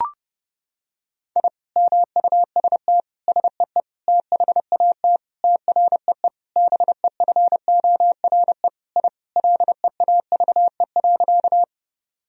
Morse code, heard as a keyed tone, about 32 wpm, 700 Hz. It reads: I MUST SEE THAT TREE BEFORE I LEAVE.
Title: Sentences from Top 1000 Words - Review Part 1 - 30wpm